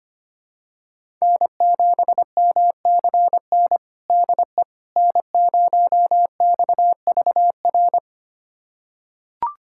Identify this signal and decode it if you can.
Morse code, heard as a keyed tone, about 25 wpm, 700 Hz. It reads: N7MCN DE N0XVR
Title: USA Callsign DE Callsign - 5 Characters - 25wpm